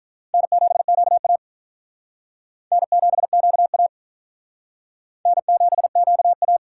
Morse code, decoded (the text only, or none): N7QA N7QA N7QA